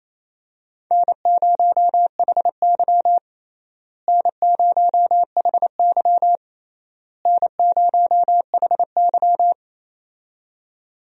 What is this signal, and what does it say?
Morse code, heard as a keyed tone, about 28 wpm, 700 Hz. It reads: N0HY N0HY N0HY
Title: USA Call Signs - 4 Characters - Repeated 3x - 28wpm